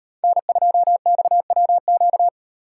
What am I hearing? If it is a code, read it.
Morse code, heard as a keyed tone, about 38 wpm, 700 Hz. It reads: N2XWQ